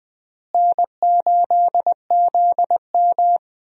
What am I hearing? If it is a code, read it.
Morse code, heard as a keyed tone, about 20 wpm, 700 Hz. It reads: N8ZM
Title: USA Call Signs - 4 Characters - Repeated 3x - 20wpm